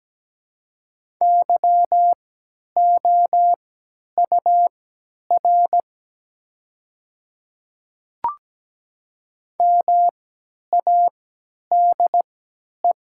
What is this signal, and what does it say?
Morse code, heard as a keyed tone, about 17 wpm, spaced as for 10 wpm, 700 Hz. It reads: YOUR MADE
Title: 3x Character Spacing - Top 200 Words - 17wpm